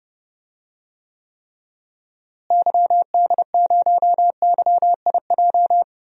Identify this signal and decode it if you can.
Morse code, heard as a keyed tone, about 31 wpm, 700 Hz. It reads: YD0YIJ
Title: DXCC-100 Call Signs - 6 Characters - Repeated 3x - 30wpm